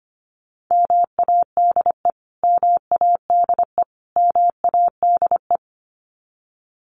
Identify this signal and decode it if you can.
Morse code, heard as a keyed tone, about 25 wpm, 700 Hz. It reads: MADE MADE MADE
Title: Top 200 Words Repeated 25wpm